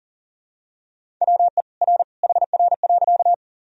Morse code, heard as a keyed tone, about 37 wpm, 700 Hz. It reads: WE R HR.
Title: Sentences from Top 200 Words and Abbreviations 40wpm